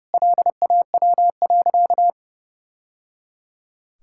Morse code, heard as a keyed tone, about 30 wpm, 700 Hz. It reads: LAW.
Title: Sentences from Top 1000 Words - Review Part 6 - 30wpm